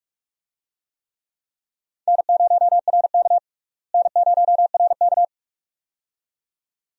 Morse code, heard as a keyed tone, about 47 wpm, 700 Hz. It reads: N0RK N0RK
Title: USA Call Signs - 4 Characters - Repeated 2x - 45wpm